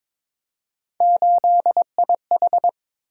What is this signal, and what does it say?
Morse code, heard as a keyed tone, about 21 wpm, 700 Hz. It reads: 8IH
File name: practice / 22wpm / Three Letter-Number Combinations - 22wpm